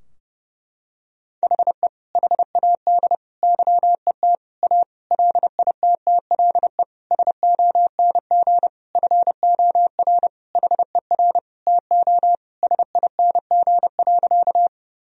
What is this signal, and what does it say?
Morse code, heard as a keyed tone, about 30 wpm, 700 Hz. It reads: HE HAD YET A LITTLE SONG FOR HER TO SING.
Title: Sentences from Top 1000 Words - Review Part 2 - 30wpm